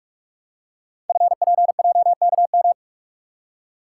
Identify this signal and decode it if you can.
Morse code, heard as a keyed tone, about 44 wpm, 700 Hz. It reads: FPJKM
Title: Code Groups - 5 Letter:Numbers - 45wpm